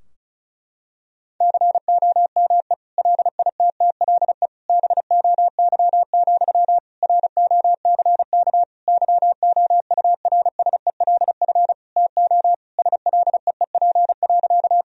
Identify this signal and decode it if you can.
Morse code, heard as a keyed tone, about 34 wpm, 700 Hz. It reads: COME LITTLE BOY, ROCK YOURSELF TO SLEEP.
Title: Sentences from Top 1000 Words - Review Part 2 - 35wpm